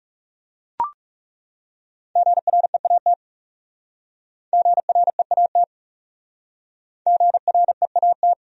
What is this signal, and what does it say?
Morse code, heard as a keyed tone, about 37 wpm, 700 Hz. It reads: GREAT GREAT GREAT
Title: Speed-Racing Top 200 Words 30wpm